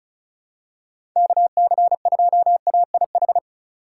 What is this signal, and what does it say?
Morse code, heard as a keyed tone, about 36 wpm, 700 Hz. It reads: KC2AIH
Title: USA Call Signs - 6 Characters - 2x Character Spacing - 35wpm